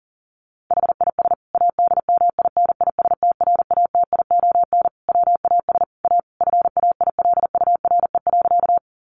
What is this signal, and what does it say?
Morse code, heard as a keyed tone, about 40 wpm, 700 Hz. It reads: HIS ADMINISTRATION WAS A FAILURE.
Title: Sentences from Top 1300 Words 40wpm